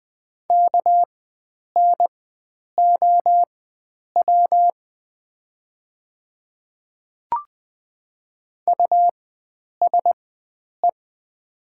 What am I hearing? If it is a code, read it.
Morse code, heard as a keyed tone, about 20 wpm, spaced as for 9 wpm, 700 Hz. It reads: KNOW USE
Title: 4x Character Spacing - Top 100 Words - 20wpm